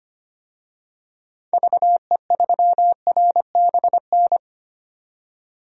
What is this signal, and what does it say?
Morse code, heard as a keyed tone, about 25 wpm, 700 Hz. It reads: VE3RBN
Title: DXCC-100 Call Signs - 6 Characters - 25wpm